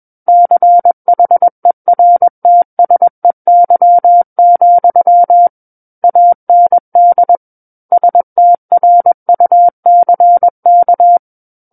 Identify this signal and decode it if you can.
Morse code, heard as a keyed tone, about 21 wpm, 700 Hz. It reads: CHERTSEY, AND STRUCK
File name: Book One / 12-Chapter 12 - 21 WPM